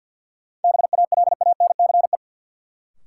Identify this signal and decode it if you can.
Morse code, heard as a keyed tone, about 53 wpm, 700 Hz. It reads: BALANCE
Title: Sentences from Top 900 Words - New Word Drill - 50wpm